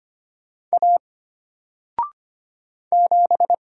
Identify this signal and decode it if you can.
Morse code, heard as a keyed tone, about 25 wpm, 700 Hz. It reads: A 7
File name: Single-Letter-Number-25wpm